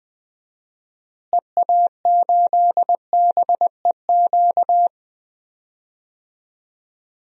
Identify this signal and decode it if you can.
Morse code, heard as a keyed tone, about 20 wpm, 700 Hz. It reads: EA8BEQ
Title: DXCC-100 Call Signs - 6 Characters - Repeated 2x - 20wpm